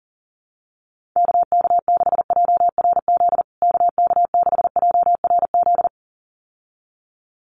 Morse code, heard as a keyed tone, about 40 wpm, 700 Hz. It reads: KK6JRZ KK6JRZ
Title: USA Call Signs Repeated 2x 40wpm